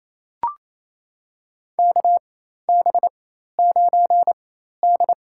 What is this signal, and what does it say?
Morse code, heard as a keyed tone, about 28 wpm, spaced as for 13 wpm, 700 Hz. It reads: KB9D
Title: USA Call Signs - 4 Characters - 4x Character Spacing - 28wpm